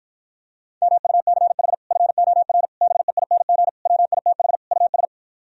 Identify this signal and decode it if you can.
Morse code, heard as a keyed tone, about 52 wpm, 700 Hz. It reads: MUCH FOR BEING WITH US